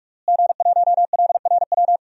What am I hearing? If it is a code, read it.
Morse code, heard as a keyed tone, about 41 wpm, 700 Hz. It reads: G1LRW
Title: DXCC-100 Call Signs - 5 Characters - Repeated 3x - 45wpm